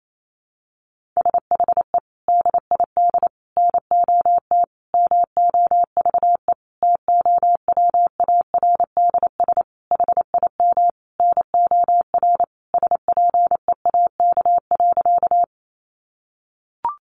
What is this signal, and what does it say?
Morse code, heard as a keyed tone, about 28 wpm, 700 Hz. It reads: SHE DID NOT MOVE TOWARDS HIM NOR SPEAK.
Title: Sentences from Top 1500 Words 28wpm